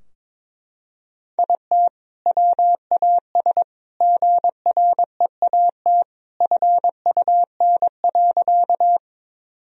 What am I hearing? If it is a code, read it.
Morse code, heard as a keyed tone, about 22 wpm, 700 Hz. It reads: IT WAS GREAT FUN.
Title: Sentences from Top 1000 Words - Review Part 4 - 22wpm